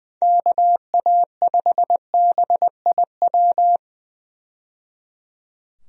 Morse code, heard as a keyed tone, about 20 wpm, 700 Hz. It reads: KA5BIW
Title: USA Call Signs - 6 Characters - 20wpm